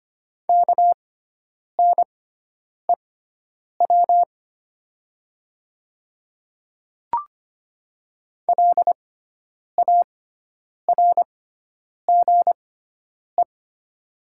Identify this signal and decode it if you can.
Morse code, heard as a keyed tone, about 25 wpm, spaced as for 9 wpm, 700 Hz. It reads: KNEW LARGE
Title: 6x Character Spacing - Top 500 Words - 25wpm